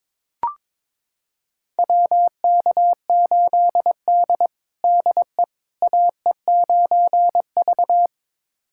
Morse code, heard as a keyed tone, about 22 wpm, 700 Hz. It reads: WK8D DE AE9V
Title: USA Callsign DE Callsign - 4 Characters - 22wpm